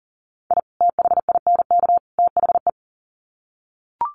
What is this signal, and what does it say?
Morse code, heard as a keyed tone, about 40 wpm, 700 Hz. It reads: I THINK THE
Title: Sets of 3 Words 40wpm